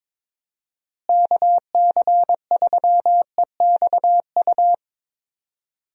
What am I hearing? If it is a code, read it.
Morse code, heard as a keyed tone, about 22 wpm, 700 Hz. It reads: KC3EXU
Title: USA Call Signs - 6 Characters - Repeated 2x - 22wpm